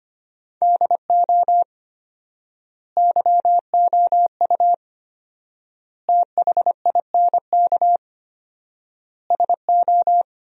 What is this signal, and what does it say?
Morse code, heard as a keyed tone, about 25 wpm, 700 Hz. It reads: DO YOU THINK SO
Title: 4x Word-Spacing - Sentences from Top 200 Words - 25wpm